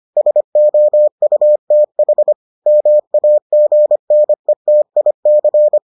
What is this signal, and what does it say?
Morse code, heard as a keyed tone, about 25 wpm, 600 Hz. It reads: SOUTH MAGNETIC